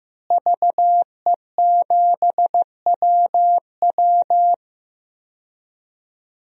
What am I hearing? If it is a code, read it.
Morse code, heard as a keyed tone, about 15 wpm, 700 Hz. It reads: VE7WW
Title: DXCC-100 Call Signs - 5 Characters - Repeated 3x - 15wpm